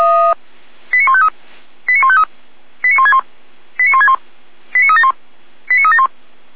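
Five-tone selective call (ZVEI), heard as quick runs of five tones, tones of about 70 ms. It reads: 89153 89153 89161 89161 89371 89371
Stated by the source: ZVEI five-tone call received by SDR, fire brigade Orlamünde (feuerwehr-orlamuende).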